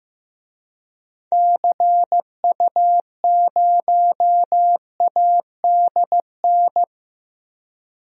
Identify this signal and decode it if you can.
Morse code, heard as a keyed tone, about 15 wpm, 700 Hz. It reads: CU0ADN
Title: DXCC-100 Call Signs - 6 Characters - Repeated 3x - 15wpm